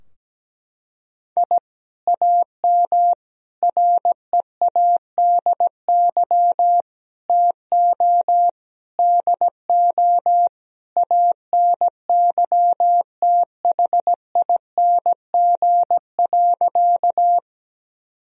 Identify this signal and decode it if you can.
Morse code, heard as a keyed tone, about 17 wpm, 700 Hz. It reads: I AM READY TO DO ANYTHING.
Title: Sentences from Top 1000 Words - Review Part 2 - 17wpm